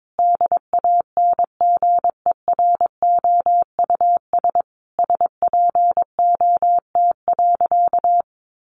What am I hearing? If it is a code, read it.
Morse code, heard as a keyed tone, about 22 wpm, 700 Hz. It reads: DANGEROUS SPOT.
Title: Sentences from Top 1200 Words 22wpm